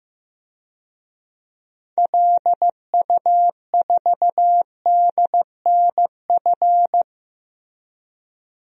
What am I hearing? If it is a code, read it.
Morse code, heard as a keyed tone, about 15 wpm, 700 Hz. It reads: LU4DNF
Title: DXCC-100 Call Signs - 6 Characters - 15wpm